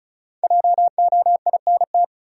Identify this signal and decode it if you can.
Morse code, heard as a keyed tone, about 34 wpm, 700 Hz. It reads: JOINT
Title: Sentences from Top 1300 Words - New Word Drill Alt - 35wpm